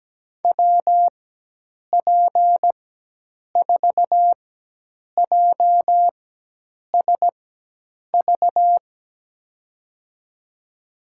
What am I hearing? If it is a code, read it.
Morse code, heard as a keyed tone, about 17 wpm, spaced as for 8 wpm, 700 Hz. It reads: WP4JSV
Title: USA Call Signs - 6 Characters - 4x Character Spacing - 17wpm